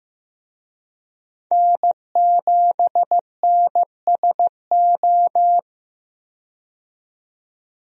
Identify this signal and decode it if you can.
Morse code, heard as a keyed tone, about 15 wpm, 700 Hz. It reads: N7NSO
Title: USA Call Signs - 5 Characters - 15wpm